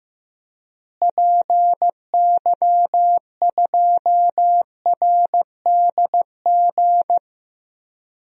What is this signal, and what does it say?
Morse code, heard as a keyed tone, about 15 wpm, 700 Hz. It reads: PY2RDG
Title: DXCC-100 Call Signs - 6 Characters - Repeated 3x - 15wpm